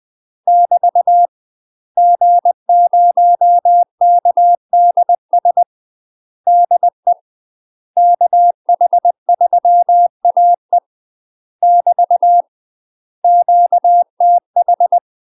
Morse code, heard as a keyed tone, about 20 wpm, 700 Hz. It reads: <BT> G0KDS DE KH3AE <BT> QTH
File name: UK MASTER QSOs TRAINING/20WPM